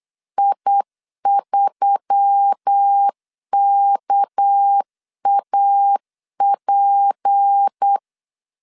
Morse code, heard as a keyed tone, about 8 wpm, 800 Hz. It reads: I3KAP
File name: LEZIONI / 6 PKX0 / 6 PK 8wpm 3003